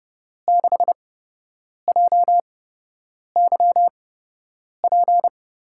Morse code, heard as a keyed tone, about 30 wpm, spaced as for 8 wpm, 700 Hz. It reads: 6JYP
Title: USA Call Signs - 8x Character Spacing - 30wpm